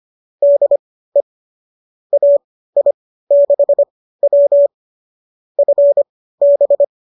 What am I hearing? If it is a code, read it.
Morse code, heard as a keyed tone, about 25 wpm, spaced as for 15 wpm, 590 Hz. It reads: DE AI6W FB